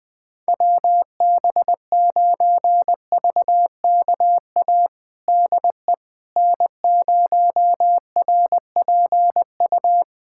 Morse code, heard as a keyed tone, about 20 wpm, 700 Hz. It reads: WB9VKA DE N0RPU
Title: USA Callsign DE Callsign 20wpm